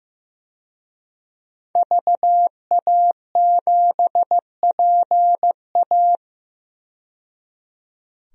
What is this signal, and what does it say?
Morse code, heard as a keyed tone, about 15 wpm, 700 Hz. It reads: VA7PA
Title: DXCC-100 Call Signs - 5 Characters - 15wpm